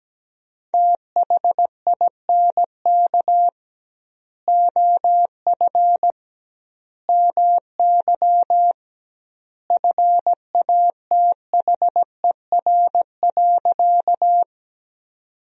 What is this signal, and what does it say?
Morse code, heard as a keyed tone, about 17 wpm, 700 Hz. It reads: THINK OF MY FATHER.
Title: 2x Word-Spacing - Sentences from Top 300 Words - 17wpm